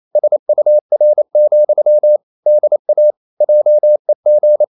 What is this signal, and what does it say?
Morse code, heard as a keyed tone, about 28 wpm, 600 Hz. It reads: SUR, DA JEG
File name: Ferie QRQ 28wpm